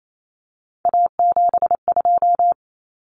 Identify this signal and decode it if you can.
Morse code, heard as a keyed tone, about 28 wpm, 700 Hz. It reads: A72
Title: Common Cut Numbers - 3-Digit 28wpm